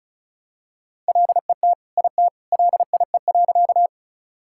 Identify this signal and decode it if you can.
Morse code, heard as a keyed tone, about 36 wpm, 700 Hz. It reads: LET IT LIE.
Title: Sentences from Top 1600 Words 35wpm